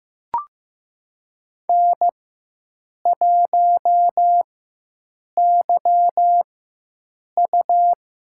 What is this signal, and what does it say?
Morse code, heard as a keyed tone, about 15 wpm, spaced as for 7 wpm, 700 Hz. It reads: N1YU
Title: USA Call Signs - 4x Character Spacing - 15wpm